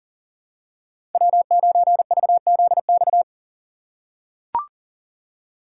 Morse code, heard as a keyed tone, about 40 wpm, 700 Hz. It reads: W9VZX
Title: USA Call Signs - 5 Characters - 3x Character Spacing - 40wpm